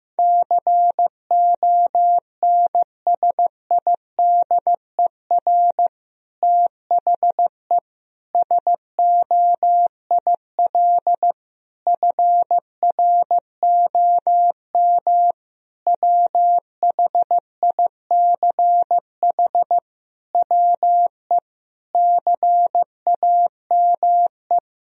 Morse code, heard as a keyed tone, about 15 wpm, 700 Hz. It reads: CONSIDER THE SOIL FROM WHICH WE CAME
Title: Sentences from Top 1000 Words - Review Part 4 - 15wpm